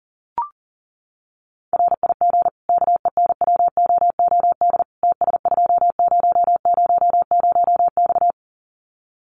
Morse code, heard as a keyed tone, about 40 wpm, 700 Hz. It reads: RIG KENWOOD TS2000X
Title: Common QSO Phrases 40wpm